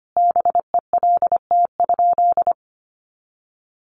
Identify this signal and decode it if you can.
Morse code, heard as a keyed tone, about 25 wpm, 700 Hz. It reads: BELT?
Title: Sentences from Top 1400 Words 25wpm